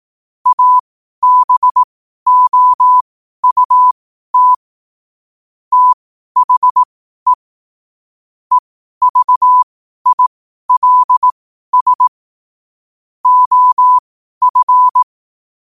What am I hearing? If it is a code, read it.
Morse code, heard as a keyed tone, about 18 wpm, spaced as for 12 wpm, 1000 Hz. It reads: ABOUT THE EVILS OF
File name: qotdfiles / 12wpm